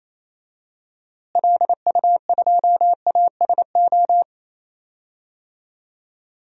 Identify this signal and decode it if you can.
Morse code, heard as a keyed tone, about 28 wpm, 700 Hz. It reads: LU2ASO
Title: DXCC-100 Call Signs - 6 Characters - Repeated 3x - 28wpm